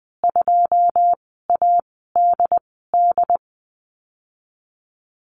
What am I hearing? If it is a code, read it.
Morse code, heard as a keyed tone, about 20 wpm, spaced as for 14 wpm, 700 Hz. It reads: 2ADD
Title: 082 Lesson 14g - Number 2 - Call Signs 20wpm